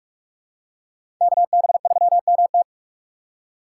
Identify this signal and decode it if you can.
Morse code, heard as a keyed tone, about 47 wpm, 700 Hz. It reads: KB3MT
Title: USA Call Signs - 6x Character Spacing - 45wpm